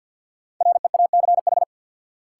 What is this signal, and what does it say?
Morse code, heard as a keyed tone, about 52 wpm, 700 Hz. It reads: REACH
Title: Sentences from Top 400 Words - New Word Drill - 50wpm